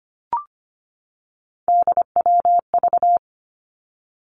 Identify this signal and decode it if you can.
Morse code, heard as a keyed tone, about 25 wpm, 700 Hz. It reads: DWV
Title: Three Challenging Letter-Number Combinations 25wpm